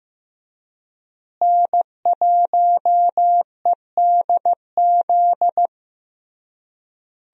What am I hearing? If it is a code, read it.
Morse code, heard as a keyed tone, about 15 wpm, 700 Hz. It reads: N1EDZ
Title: USA Call Signs - 5 Characters - Repeated 3x - 15wpm